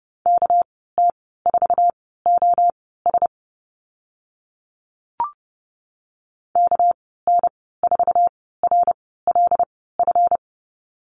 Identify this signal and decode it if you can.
Morse code, heard as a keyed tone, about 30 wpm, spaced as for 17 wpm, 700 Hz. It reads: KT4OS KN4RLF
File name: Lesson 37g - Letter x - Call Signs 30wpm